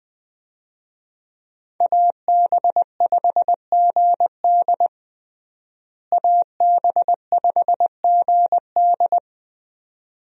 Morse code, heard as a keyed tone, about 21 wpm, 700 Hz. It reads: AB5GD AB5GD
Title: USA Call Signs - 5 Characters - Repeated 2x - 20wpm